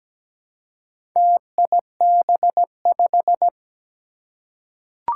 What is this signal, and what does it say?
Morse code, heard as a keyed tone, about 17 wpm, 700 Hz. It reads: TIB5
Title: Four Challenging Letter-Number Combinations 17wpm